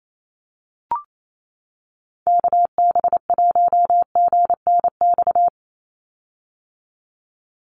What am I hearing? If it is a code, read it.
Morse code, heard as a keyed tone, about 28 wpm, 700 Hz. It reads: KB1GNX
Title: USA Call Signs 28wpm